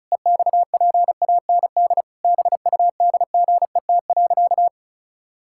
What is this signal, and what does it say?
Morse code, heard as a keyed tone, about 36 wpm, 700 Hz. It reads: EXPAND BUDGET.
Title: Pangram Challenge v2 - 35wpm